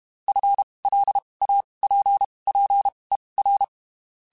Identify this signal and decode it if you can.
Morse code, heard as a keyed tone, about 32 wpm, spaced as for 23 wpm, 800 Hz. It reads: FLAPPER
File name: Spaced / FL Runs sp mp3s / FL Runs sp @30